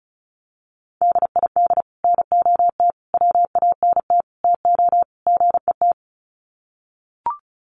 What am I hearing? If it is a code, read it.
Morse code, heard as a keyed tone, about 34 wpm, 700 Hz. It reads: DID NOT WANT TO GET
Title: Sets of 5 Words Encore2 35wpm